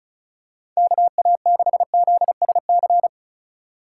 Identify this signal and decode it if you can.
Morse code, heard as a keyed tone, about 36 wpm, 700 Hz. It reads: KA6ZSC
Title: USA Call Signs - 6 Characters - 3x Character Spacing - 35wpm